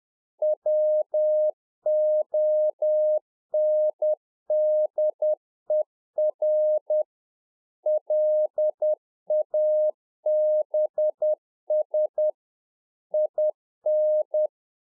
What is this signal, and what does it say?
Morse code, heard as a keyed tone, about 10 wpm, 610 Hz. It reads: WONDER LABS IN